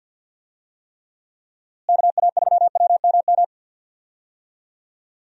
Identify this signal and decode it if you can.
Morse code, heard as a keyed tone, about 53 wpm, 700 Hz. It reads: KA3WMM
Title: USA Call Signs - 6 Characters - Repeated 2x - 50wpm